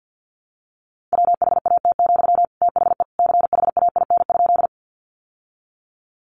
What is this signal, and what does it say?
Morse code, heard as a keyed tone, about 50 wpm, 700 Hz. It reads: WHAT, THE CHAIN?
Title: Sentences from Top 1500 Words 50wpm